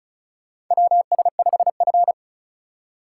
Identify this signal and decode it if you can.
Morse code, heard as a keyed tone, about 36 wpm, 700 Hz. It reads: WS5F
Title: USA Call Signs - 3x Character Spacing - 35wpm